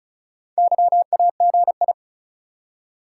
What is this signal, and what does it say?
Morse code, heard as a keyed tone, about 36 wpm, 700 Hz. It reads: YAGI
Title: Top 180 QSO Elements 35wpm